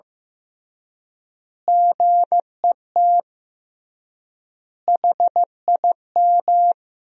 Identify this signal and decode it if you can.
Morse code, heard as a keyed tone, about 15 wpm, 700 Hz. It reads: GET HIM
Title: 3x Word-Spacing - Sentences from Top 400 Words - 15wpm